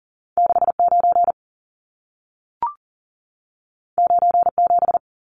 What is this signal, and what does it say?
Morse code, heard as a keyed tone, about 40 wpm, 700 Hz. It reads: 69 97
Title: Numbers - 2-Digit 40wpm